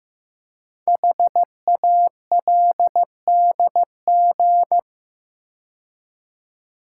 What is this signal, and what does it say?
Morse code, heard as a keyed tone, about 15 wpm, 700 Hz. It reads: HALDG
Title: Code Groups - 5 Letters - 15wpm